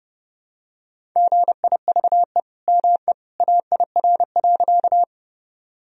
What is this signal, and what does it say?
Morse code, heard as a keyed tone, about 30 wpm, 700 Hz. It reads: GIVE ME AIR.
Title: Sentences from Top 1000 Words - Review Part 1 - 30wpm